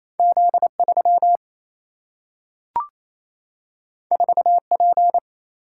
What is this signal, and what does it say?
Morse code, heard as a keyed tone, about 28 wpm, 700 Hz. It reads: Z 3 4 P
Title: Single-Letter Single-Number 28wpm